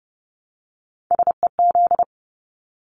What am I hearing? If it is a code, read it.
Morse code, heard as a keyed tone, about 30 wpm, 700 Hz. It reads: SEZ
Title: Abbreviations Repeated 3x - 30wpm